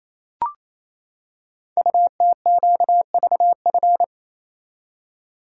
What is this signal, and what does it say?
Morse code, heard as a keyed tone, about 27 wpm, 700 Hz. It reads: UTQVF
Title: Code Groups - 5 Letters - 28wpm